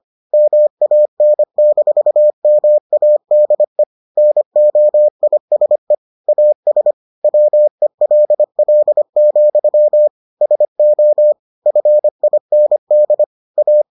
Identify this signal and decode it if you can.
Morse code, heard as a keyed tone, about 25 wpm, 600 Hz. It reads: MAN-MADE NOISE AS WELL, SO FIND A